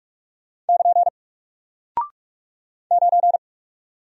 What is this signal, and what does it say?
Morse code, heard as a keyed tone, about 46 wpm, 700 Hz. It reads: <KN> 9
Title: Single Character (Letter:Number:Punctuation:Prosign) Alt 45wpm